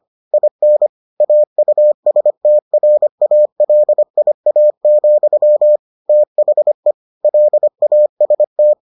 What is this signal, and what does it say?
Morse code, heard as a keyed tone, about 25 wpm, 600 Hz. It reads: IN AUSTRALIA, THE LAST